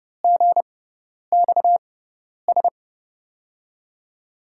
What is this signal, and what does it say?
Morse code, heard as a keyed tone, about 32 wpm, spaced as for 10 wpm, 700 Hz. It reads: GXS
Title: USA Call Signs - 6 Characters - 6x Character Spacing - 30wpm